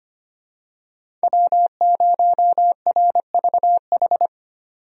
Morse code, heard as a keyed tone, about 25 wpm, 700 Hz. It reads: W0RVH
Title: USA Call Signs - 5 Characters - 25wpm